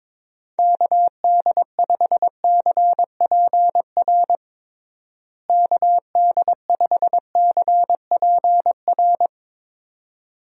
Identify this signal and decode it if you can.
Morse code, heard as a keyed tone, about 22 wpm, 700 Hz. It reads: KD5CPR KD5CPR
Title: USA Call Signs - 6 Characters - Repeated 2x - 22wpm